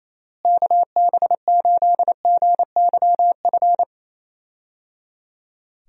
Morse code, heard as a keyed tone, about 28 wpm, 700 Hz. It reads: KB8GYF
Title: USA Call Signs - 6 Characters - Repeated 3x - 28wpm